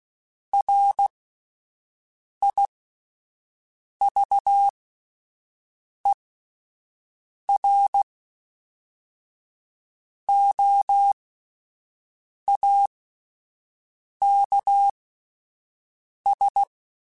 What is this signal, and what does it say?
Morse code, heard as a keyed tone, about 16 wpm, spaced as for 5 wpm, 800 Hz. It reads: RIVEROAKS